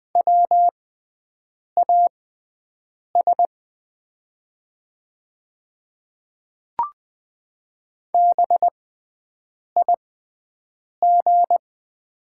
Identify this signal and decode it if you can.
Morse code, heard as a keyed tone, about 20 wpm, spaced as for 7 wpm, 700 Hz. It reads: WAS BIG
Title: 6x Character Spacing - Top 200 Words - 20wpm